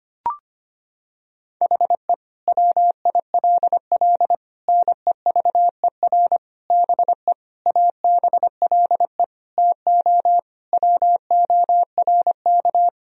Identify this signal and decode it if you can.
Morse code, heard as a keyed tone, about 25 wpm, 700 Hz. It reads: HE WILL NEVER BE ABLE TO WORK
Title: Sentences from Top 1000 Words - Review Part 2 - 25wpm